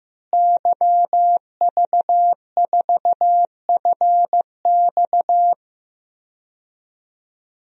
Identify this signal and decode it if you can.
Morse code, heard as a keyed tone, about 15 wpm, 700 Hz. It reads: YV4FX
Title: DXCC-100 Call Signs - 5 Characters - Repeated 3x - 15wpm